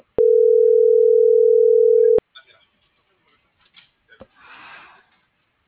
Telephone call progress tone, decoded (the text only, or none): ringback tone